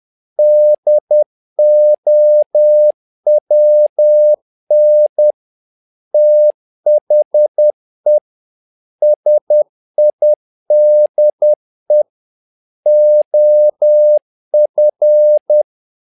Morse code, heard as a keyed tone, about 10 wpm, 600 Hz. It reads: DOWN THE SIDE OF
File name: learning center / tower trouble / tower trouble 10wpm